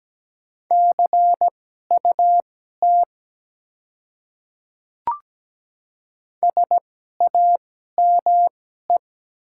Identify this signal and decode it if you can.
Morse code, heard as a keyed tone, about 17 wpm, spaced as for 12 wpm, 700 Hz. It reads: CUT SAME
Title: 2x Character Spacing - Top 300 Words - 17wpm